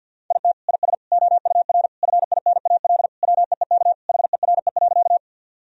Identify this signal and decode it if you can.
Morse code, heard as a keyed tone, about 52 wpm, 700 Hz. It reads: IT IS OUR FINAL WEEK HERE.
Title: Sentences from Top 1000 Words - Review Part 2 - 50wpm